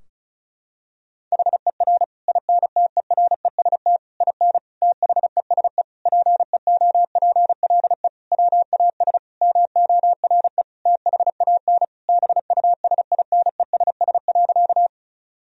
Morse code, heard as a keyed tone, about 36 wpm, 700 Hz. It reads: HER INTEREST IN THESE PEOPLE WAS MORE THAN BUSINESS.
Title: Sentences from Top 1000 Words - Review Part 2 - 35wpm